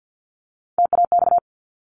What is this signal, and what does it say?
Morse code, heard as a keyed tone, about 50 wpm, 700 Hz. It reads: TAX
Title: Top 501-750 Words 50wpm